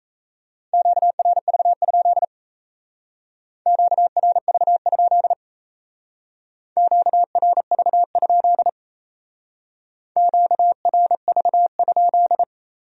Morse code, heard as a keyed tone, about 34 wpm, 700 Hz. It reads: QRV? QRV? QRV? QRV?
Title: Speed-Racing Q-Codes - 28wpm